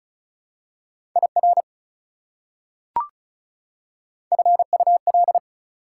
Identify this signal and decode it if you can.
Morse code, heard as a keyed tone, about 36 wpm, 700 Hz. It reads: IR FUL
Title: Prefixes and Suffixes 35wpm